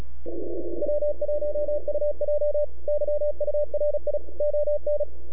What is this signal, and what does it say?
Morse code, heard as a keyed tone, about 36 wpm, 590 Hz, with radio noise. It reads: W1UJ YURI ON
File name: NAQPCW-8-2009 / W1UJ NAQP CW 09 2230Z